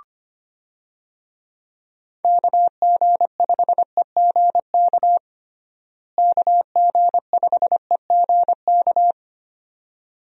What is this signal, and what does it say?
Morse code, heard as a keyed tone, about 25 wpm, 700 Hz. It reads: KG5EGK KG5EGK
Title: USA Call Signs - 6 Characters - Repeated 2x - 25wpm